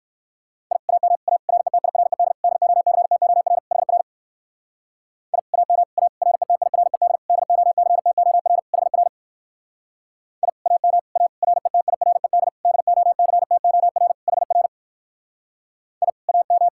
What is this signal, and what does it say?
Morse code, heard as a keyed tone, about 55 wpm, 700 Hz. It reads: I AM A RETIRED DOCTOR HR I AM A RETIRED DOCTOR HR I AM A RETIRED DOCTOR HR I AM
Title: Speed-Racing Common QSO Phrases Encore - 45wpm